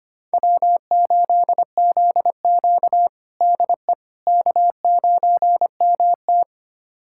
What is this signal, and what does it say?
Morse code, heard as a keyed tone, about 25 wpm, 700 Hz. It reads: W8ZQ DE K9MT
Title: USA Callsign DE Callsign - 4 Characters - 25wpm